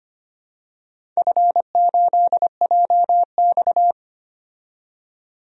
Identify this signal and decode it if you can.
Morse code, heard as a keyed tone, about 25 wpm, 700 Hz. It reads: F8JX
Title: DXCC-100 Call Signs - 4 Characters - 25wpm